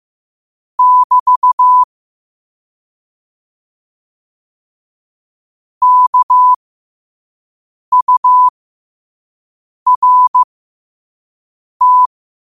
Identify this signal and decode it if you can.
Morse code, heard as a keyed tone, about 15 wpm, spaced as for 5 wpm, 1000 Hz. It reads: <BT> KURT
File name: qotdfiles / 5wpm